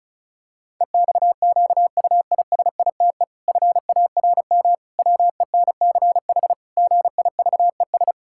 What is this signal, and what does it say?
Morse code, heard as a keyed tone, about 36 wpm, 700 Hz. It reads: EXQUISITE FARM WENCH GIVES
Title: Pangram Challenge v2 - 35wpm